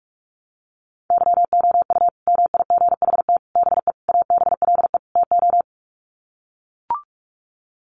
Morse code, heard as a keyed tone, about 44 wpm, 700 Hz. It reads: YOU MIGHT BE ABLE TO
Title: Sets of 5 Words Encore2 45wpm